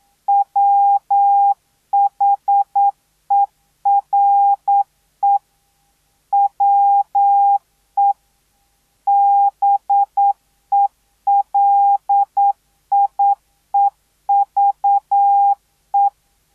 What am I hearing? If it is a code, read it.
Morse code, heard as a keyed tone, about 9 wpm, 800 Hz. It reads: WHERE WE BELIEVE